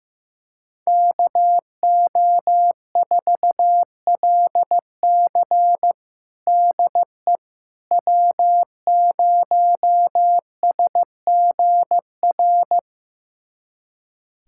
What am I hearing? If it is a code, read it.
Morse code, heard as a keyed tone, about 15 wpm, 700 Hz. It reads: KO4LC DE W0SGR
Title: USA Callsign DE Callsign - 5 Characters - 15wpm